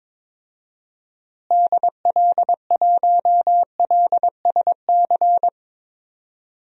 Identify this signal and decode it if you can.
Morse code, heard as a keyed tone, about 22 wpm, 700 Hz. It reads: DL1LSC
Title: DXCC-100 Call Signs - 6 Characters - Repeated 2x - 22wpm